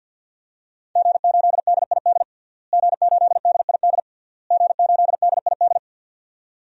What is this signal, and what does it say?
Morse code, heard as a keyed tone, about 50 wpm, 700 Hz. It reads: G8DID G8DID G8DID
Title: DXCC-100 Call Signs - 5 Characters - Repeated 3x - 50wpm